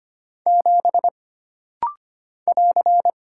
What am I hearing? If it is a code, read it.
Morse code, heard as a keyed tone, about 25 wpm, 700 Hz. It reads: 7 <AR>
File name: Single Character (Letter:Number:Punctuation:Prosign) - Rapid-Fire - 25wpm